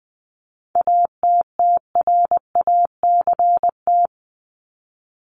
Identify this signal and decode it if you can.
Morse code, heard as a keyed tone, about 20 wpm, 700 Hz. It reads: ATTRACT
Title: Top 1501-1750 Words 20wpm